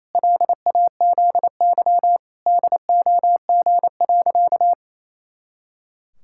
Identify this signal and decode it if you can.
Morse code, heard as a keyed tone, about 28 wpm, 700 Hz. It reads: LAZY DOG.
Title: Pangram Challenge v2 - 28wpm